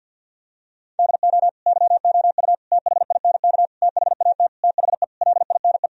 Morse code, heard as a keyed tone, about 50 wpm, 700 Hz. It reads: DO YOU THINK THAT THE LINE
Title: Sentences from Top 1000 Words - Review Part 1 - 50wpm